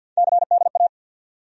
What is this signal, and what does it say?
Morse code, heard as a keyed tone, about 52 wpm, 700 Hz. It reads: CDA
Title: USA Call Sign Postfixes - 3 Letters 50wpm